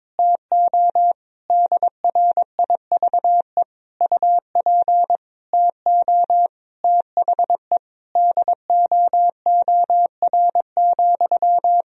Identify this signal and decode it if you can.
Morse code, heard as a keyed tone, about 22 wpm, 700 Hz. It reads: TO DRIVE UP TO THE DOOR,
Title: Sentences from Top 1000 Words - Review Part 2 - 22wpm